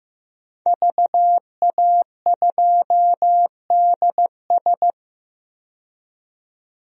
Morse code, heard as a keyed tone, about 15 wpm, 700 Hz. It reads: VA2DS
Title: DXCC-100 Call Signs - 5 Characters - 15wpm